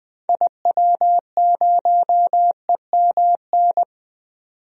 Morse code, heard as a keyed tone, about 20 wpm, 700 Hz. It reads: IW0EMN
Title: DXCC-100 Call Signs - 6 Characters - 20wpm